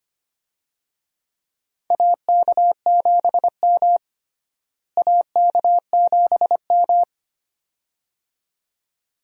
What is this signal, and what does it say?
Morse code, heard as a keyed tone, about 25 wpm, 700 Hz. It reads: AK7M AK7M
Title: USA Call Signs - 4 Characters - Repeated 2x - 25wpm